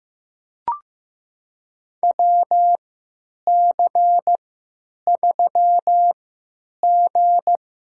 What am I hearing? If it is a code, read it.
Morse code, heard as a keyed tone, about 15 wpm, spaced as for 9 wpm, 700 Hz. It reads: WC3G
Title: USA Call Signs - 4 Characters - 3x Character Spacing - 15wpm